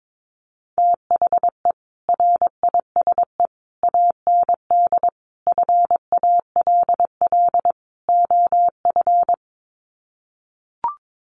Morse code, heard as a keyed tone, about 22 wpm, 700 Hz. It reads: THE RISE AND FALL OF
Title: Sets of 5 Words Encore2 22wpm